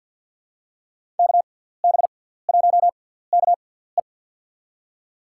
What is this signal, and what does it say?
Morse code, heard as a keyed tone, about 49 wpm, spaced as for 17 wpm, 700 Hz. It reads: KB1KE
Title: USA Call Signs - 5 Characters - 6x Character Spacing - 50wpm